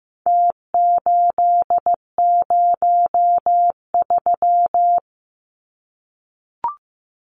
Morse code, heard as a keyed tone, about 15 wpm, 700 Hz. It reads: T803
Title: Numbers - Contest Style - 4-Digit 15wpm